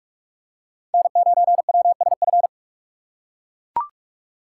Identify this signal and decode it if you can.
Morse code, heard as a keyed tone, about 42 wpm, 700 Hz. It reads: N9WSF
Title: USA Call Signs - 6x Character Spacing - 45wpm